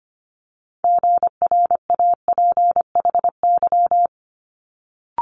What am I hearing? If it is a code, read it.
Morse code, heard as a keyed tone, about 25 wpm, 700 Hz. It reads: GRAPHY
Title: Multi-letter Phonemes Quick Fire 25wpm